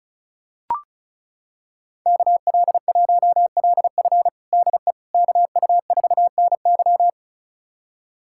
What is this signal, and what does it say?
Morse code, heard as a keyed tone, about 34 wpm, 700 Hz. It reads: KL1LF DE KU4NY
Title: USA Callsign DE Callsign - 5 Characters - 35wpm